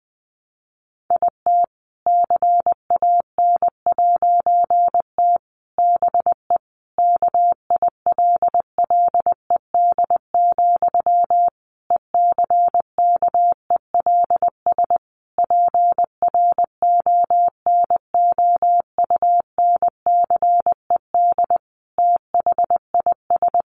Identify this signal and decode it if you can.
Morse code, heard as a keyed tone, about 20 wpm, 700 Hz. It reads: IT CAN'T BE KILLED, ECKELS PRONOUNCED THIS
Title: A Sound Of Thunder 20wpm